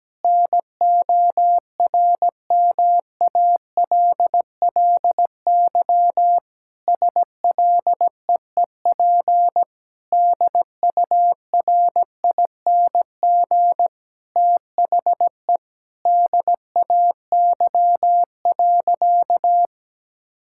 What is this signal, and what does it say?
Morse code, heard as a keyed tone, about 17 wpm, 700 Hz. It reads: NORMALLY SLEEP DURING THE DAY.
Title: Sentences from Top 1000 Words - Review Part 6 - 17wpm